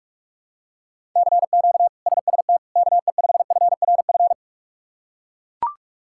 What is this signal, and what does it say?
Morse code, heard as a keyed tone, about 41 wpm, 700 Hz. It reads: CQ SST KE5FRF
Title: K1USN SST QSO Practice - 45wpm